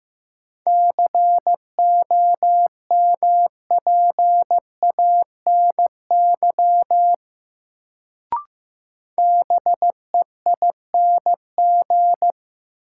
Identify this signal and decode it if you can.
Morse code, heard as a keyed tone, about 15 wpm, 700 Hz. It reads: COMPANY BEING
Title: Sentences from Top 600 Words - New Word Drill - 15wpm